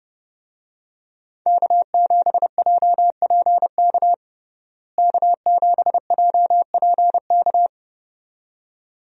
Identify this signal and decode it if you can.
Morse code, heard as a keyed tone, about 30 wpm, 700 Hz. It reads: K7JPK K7JPK
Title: USA Call Signs - 5 Characters - Repeated 2x - 30wpm